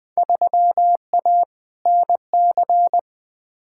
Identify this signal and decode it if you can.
Morse code, heard as a keyed tone, about 20 wpm, 700 Hz. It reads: 3A NC
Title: Field Day Exchanges - 20wpm